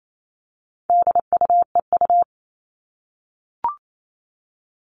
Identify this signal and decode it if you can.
Morse code, heard as a keyed tone, about 28 wpm, 700 Hz. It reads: DUEU
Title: All Cut Numbers - 4-Digit 28wpm